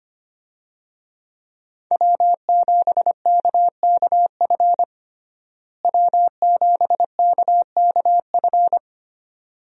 Morse code, heard as a keyed tone, about 25 wpm, 700 Hz. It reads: W7KKF W7KKF
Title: USA Call Signs - 5 Characters - Repeated 2x - 25wpm